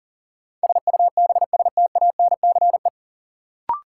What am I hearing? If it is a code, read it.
Morse code, heard as a keyed tone, about 40 wpm, 700 Hz. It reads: SUBSTANCE
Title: Sentences from Top 1500 Words - New Word Drill Alt - 40wpm